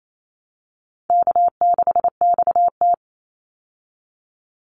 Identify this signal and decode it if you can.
Morse code, heard as a keyed tone, about 28 wpm, 700 Hz. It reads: K6XT
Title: Easy Call Signs Repeated 3x - 28wpm